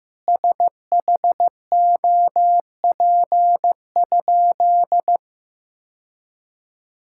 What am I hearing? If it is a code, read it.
Morse code, heard as a keyed tone, about 15 wpm, 700 Hz. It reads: SHOP?
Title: Sentences from Top 1600 Words 15wpm